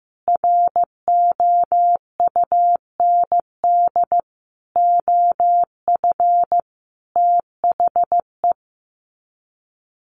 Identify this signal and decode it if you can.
Morse code, heard as a keyed tone, about 15 wpm, 700 Hz. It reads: ROUND OF THE
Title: Sets of 5 Words Encore 15wpm